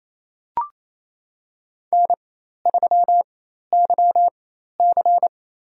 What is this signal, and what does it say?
Morse code, heard as a keyed tone, about 28 wpm, spaced as for 13 wpm, 700 Hz. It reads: N3YC
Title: USA Call Signs - 4x Character Spacing - 28wpm